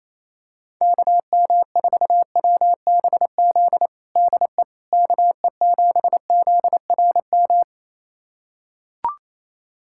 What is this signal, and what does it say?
Morse code, heard as a keyed tone, about 28 wpm, 700 Hz. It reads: KM4WBZ DE KE7ZRM
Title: USA Callsign DE Callsign 28wpm